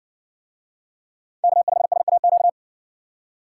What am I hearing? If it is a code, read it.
Morse code, heard as a keyed tone, about 60 wpm, 700 Hz. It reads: K5SAQ